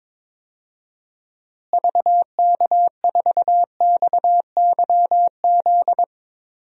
Morse code, heard as a keyed tone, about 22 wpm, 700 Hz. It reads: VK4XYZ
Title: DXCC-100 Call Signs - 6 Characters - Repeated 3x - 22wpm